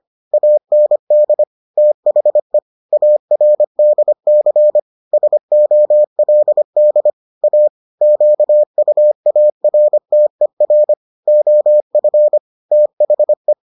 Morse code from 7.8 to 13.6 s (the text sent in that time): QUARTER OF THE